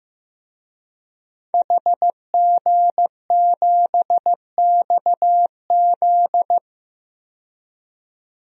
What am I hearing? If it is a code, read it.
Morse code, heard as a keyed tone, about 16 wpm, 700 Hz. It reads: HG7XZ
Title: DXCC-100 Call Signs - 5 Characters - 15wpm